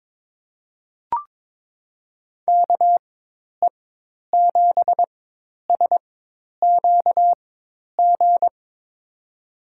Morse code, heard as a keyed tone, about 22 wpm, spaced as for 10 wpm, 700 Hz. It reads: KE7SQG
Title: USA Call Signs - 6 Characters - 4x Character Spacing - 22wpm